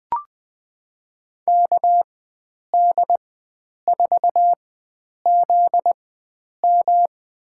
Morse code, heard as a keyed tone, about 20 wpm, spaced as for 9 wpm, 700 Hz. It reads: KD4ZM
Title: USA Call Signs - 4x Character Spacing - 20wpm